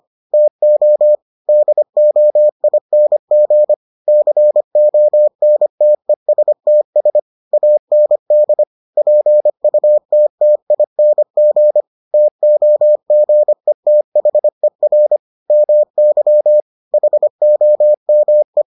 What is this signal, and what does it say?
Morse code, heard as a keyed tone, about 25 wpm, 600 Hz. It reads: TO DOING CONTESTS AND PUTTING TOGETHER MY HOME